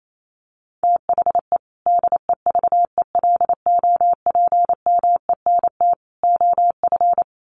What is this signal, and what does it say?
Morse code, heard as a keyed tone, about 28 wpm, 700 Hz. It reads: THE DEVELOPMENT OF
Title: Sets of 3 Words 28wpm